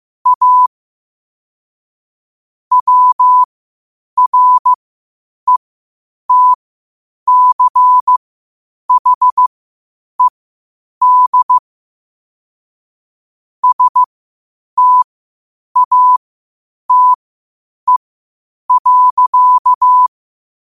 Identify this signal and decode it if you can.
Morse code, heard as a keyed tone, about 15 wpm, spaced as for 8 wpm, 1000 Hz. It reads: A WRETCHED STATE.
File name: qotdfiles / 8wpm